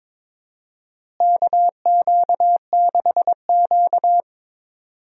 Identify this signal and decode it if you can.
Morse code, heard as a keyed tone, about 22 wpm, 700 Hz. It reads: KQ6Q
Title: USA Call Signs - 4 Characters - Repeated 3x - 22wpm